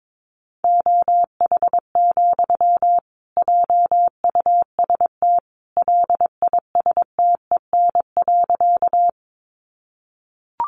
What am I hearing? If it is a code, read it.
Morse code, heard as a keyed tone, about 22 wpm, 700 Hz. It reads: OH, JUST LISTEN.
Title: Sentences from Top 500 Words 22wpm